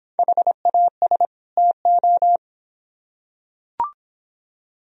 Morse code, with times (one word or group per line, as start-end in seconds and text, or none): 0.2-1.3 HAS
1.6-2.4 TO